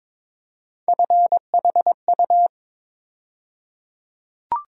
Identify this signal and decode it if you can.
Morse code, heard as a keyed tone, about 22 wpm, 700 Hz. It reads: FHU
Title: USA Call Sign Postfixes - 3 Letters 22wpm